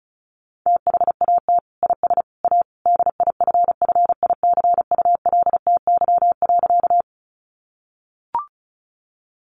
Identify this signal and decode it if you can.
Morse code, heard as a keyed tone, about 34 wpm, 700 Hz. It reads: THAT IS A DIFFICULTY.
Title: Sentences from Top 1500 Words 35wpm